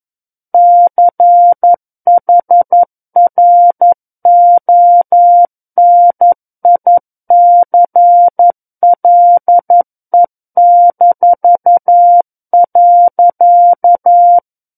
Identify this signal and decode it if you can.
Morse code, heard as a keyed tone, about 11 wpm, 700 Hz. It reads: CHRONICLE-.
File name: 11 WPM